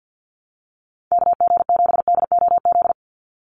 Morse code, heard as a keyed tone, about 50 wpm, 700 Hz. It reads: KG7DOZ
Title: USA Call Signs 50wpm